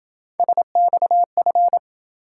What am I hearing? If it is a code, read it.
Morse code, heard as a keyed tone, about 26 wpm, 700 Hz. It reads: SXF